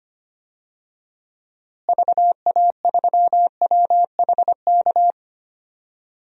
Morse code, heard as a keyed tone, about 25 wpm, 700 Hz. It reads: VA3WHK
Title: DXCC-100 Call Signs - 6 Characters - Repeated 2x - 25wpm